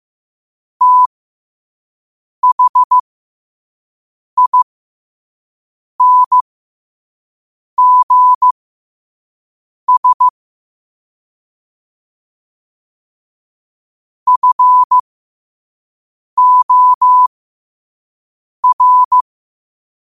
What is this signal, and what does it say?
Morse code, heard as a keyed tone, about 15 wpm, spaced as for 5 wpm, 1000 Hz. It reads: THINGS FOR
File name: qotdfiles / 5wpm